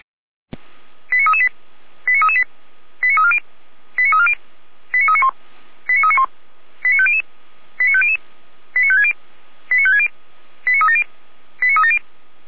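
Five-tone selective call (ZVEI), heard as quick runs of five tones, tones of about 70 ms. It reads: 89338 89338 89340 89340 89391 89391 89500 89500 89560 89560 89370 89370
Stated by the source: ZVEI five-tone selective call 89325 received on an SDR.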